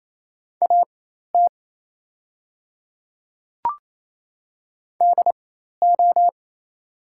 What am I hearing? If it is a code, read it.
Morse code, heard as a keyed tone, about 27 wpm, spaced as for 13 wpm, 690 Hz. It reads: AT DO
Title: 4x Character Spacing - Top 100 Words - 28wpm